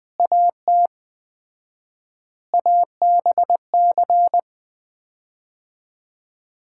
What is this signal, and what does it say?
Morse code, heard as a keyed tone, about 20 wpm, 700 Hz. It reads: AT ABC
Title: 4x Word-Spacing - Sets of 5 Words - 20wpm